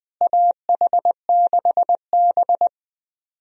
Morse code, heard as a keyed tone, about 20 wpm, 700 Hz. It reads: AH6B
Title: USA Call Signs - 4 Characters - Repeated 2x - 20wpm